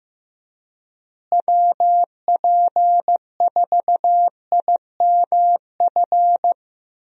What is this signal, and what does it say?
Morse code, heard as a keyed tone, about 15 wpm, 700 Hz. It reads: WP4IMF
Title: USA Call Signs - 6 Characters - Repeated 2x - 15wpm